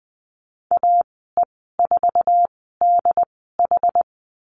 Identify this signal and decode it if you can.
Morse code, heard as a keyed tone, about 20 wpm, spaced as for 14 wpm, 700 Hz. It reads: AE4DH
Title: 082 Lesson 14g - Number 2 - Call Signs 20wpm